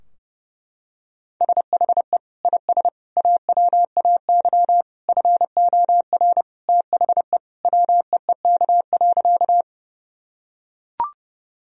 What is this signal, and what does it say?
Morse code, heard as a keyed tone, about 32 wpm, 700 Hz. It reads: SHE IS AWAY FOR THE WEEK.
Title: Sentences from Top 1000 Words - Review Part 2 - 30wpm